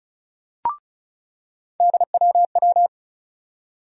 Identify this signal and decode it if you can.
Morse code, heard as a keyed tone, about 36 wpm, 700 Hz. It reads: DWW
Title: USA Call Sign Postfixes - 3 Letters 35wpm